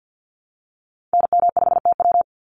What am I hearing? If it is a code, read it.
Morse code, heard as a keyed tone, about 48 wpm, 700 Hz. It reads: NM5TW
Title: NA SOTA Activator Callsigns 50wpm